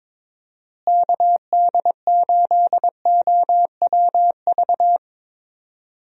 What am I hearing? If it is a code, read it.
Morse code, heard as a keyed tone, about 21 wpm, 700 Hz. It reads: KD8OWV
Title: USA Call Signs - 6 Characters - Repeated 2x - 22wpm